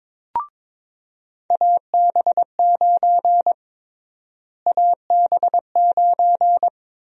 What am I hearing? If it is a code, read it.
Morse code, heard as a keyed tone, about 21 wpm, 700 Hz. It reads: AB9 AB9
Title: USA Call Sign Prefixes - Repeated 2x - 22wpm